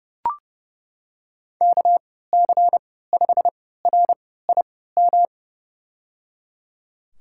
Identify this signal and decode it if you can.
Morse code, heard as a keyed tone, about 30 wpm, spaced as for 17 wpm, 700 Hz. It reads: KC5RIM
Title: USA Call Signs - 3x Character Spacing - 30wpm